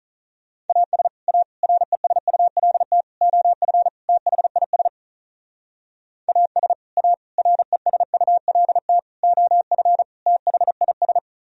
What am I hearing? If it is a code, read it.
Morse code, heard as a keyed tone, about 44 wpm, 700 Hz. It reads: AS A RESULT OF THIS AS A RESULT OF THIS